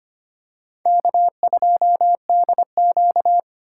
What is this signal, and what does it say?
Morse code, heard as a keyed tone, about 25 wpm, 700 Hz. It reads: K2DQ
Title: USA Call Signs - 4 Characters - 4x Character Spacing - 25wpm